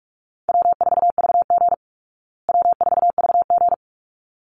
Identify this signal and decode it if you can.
Morse code, heard as a keyed tone, about 44 wpm, 700 Hz. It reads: W4VG W4VG
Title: Easy Call Signs Repeated 2x - 45wpm